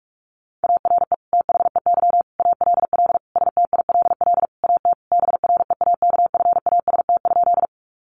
Morse code, heard as a keyed tone, about 44 wpm, 700 Hz. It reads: ARE THEY ALL STILL AT BREAKFAST?
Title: Sentences from Top 1500 Words 45wpm